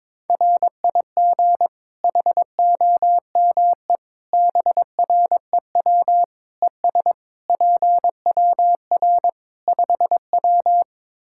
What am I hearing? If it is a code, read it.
Morse code, heard as a keyed tone, about 22 wpm, 700 Hz. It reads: RIG HOME BREW ES PWR 5W
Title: K1USN SST QSO Practice 22wpm